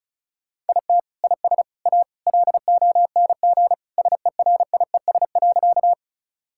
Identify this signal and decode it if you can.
Morse code, heard as a keyed tone, about 36 wpm, 700 Hz. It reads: IT IS A LONG SERIES.
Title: Sentences from Top 1000 Words - Review Part 5 - 35wpm